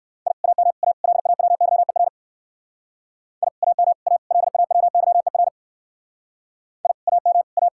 Morse code, heard as a keyed tone, about 54 wpm, 700 Hz. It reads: I AM A LAWYER I AM A LAWYER I AM A
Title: Speed-Racing Common QSO Phrases - 45wpm